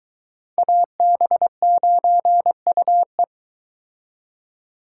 Morse code, heard as a keyed tone, about 23 wpm, 700 Hz. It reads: AB9UE